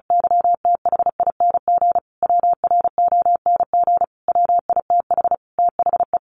Morse code, heard as a keyed tone, about 34 wpm, 700 Hz. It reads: YTHING WRONG WITH THE
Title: Sentences from Top 1500 Words 35wpm